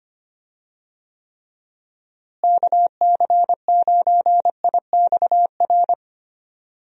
Morse code, heard as a keyed tone, about 25 wpm, 700 Hz. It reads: KC9IXR
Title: USA Call Signs - 6 Characters - Repeated 3x - 25wpm